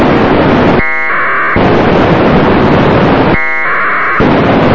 Packet radio (AX.25, as in APRS).
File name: no-44 pcsat